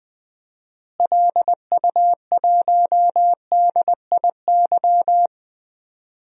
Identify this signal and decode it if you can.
Morse code, heard as a keyed tone, about 20 wpm, 700 Hz. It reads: LU1DIY
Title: DXCC-100 Call Signs - 6 Characters - Repeated 3x - 20wpm